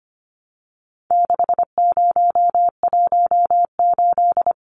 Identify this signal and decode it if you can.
Morse code, heard as a keyed tone, about 25 wpm, 700 Hz. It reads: 6018
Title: Numbers - 4-Digit 25wpm